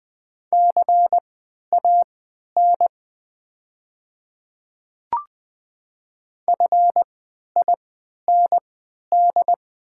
Morse code, heard as a keyed tone, about 20 wpm, spaced as for 11 wpm, 690 Hz. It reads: CAN FIND
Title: 3x Character Spacing - Top 200 Words - 20wpm